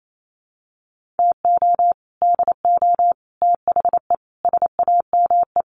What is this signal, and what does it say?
Morse code, heard as a keyed tone, about 28 wpm, 700 Hz. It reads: TO DO THE SAME
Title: Sets of 5 Words Encore 28wpm